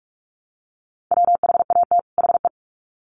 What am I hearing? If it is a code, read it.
Morse code, heard as a keyed tone, about 44 wpm, 700 Hz. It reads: WHAT HE
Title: Sets of 2 Words 45wpm